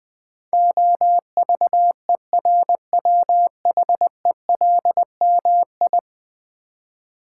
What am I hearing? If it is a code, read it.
Morse code, heard as a keyed tone, about 21 wpm, 700 Hz. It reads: OVERWHELMI
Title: Build A Word - 5 Letters At A Time - 12-Letter Words - 20wpm